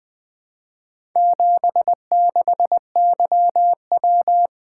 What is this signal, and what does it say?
Morse code, heard as a keyed tone, about 20 wpm, 700 Hz. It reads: 76YW
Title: Four Letter-Number Combinations - 20wpm